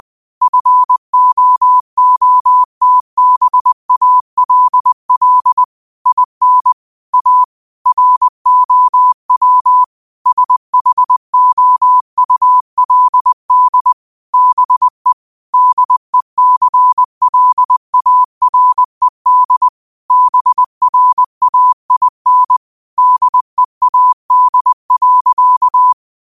Morse code, heard as a keyed tone, about 20 wpm, 1000 Hz. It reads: FOOTBALL IN A ROW SHOULD BE DECLARED BRAIN DEAD.
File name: qotdfiles / 20wpm